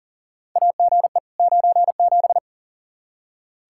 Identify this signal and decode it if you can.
Morse code, heard as a keyed tone, about 37 wpm, 700 Hz. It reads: AGE 97
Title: Practice QSO - Exchange 3 - More Info - 40wpm